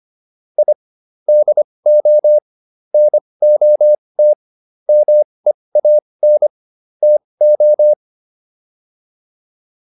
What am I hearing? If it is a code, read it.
Morse code, heard as a keyed tone, about 25 wpm, spaced as for 20 wpm, 600 Hz. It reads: I DO NOT MEAN TO